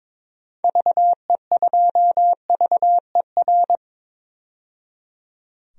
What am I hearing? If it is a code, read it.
Morse code, heard as a keyed tone, about 22 wpm, 700 Hz. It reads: VE2VER
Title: DXCC-100 Call Signs - 6 Characters - 22wpm